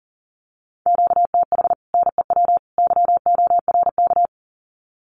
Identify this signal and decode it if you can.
Morse code, heard as a keyed tone, about 38 wpm, 700 Hz. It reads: QTH NEW YORK
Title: Common QSO Phrases 40wpm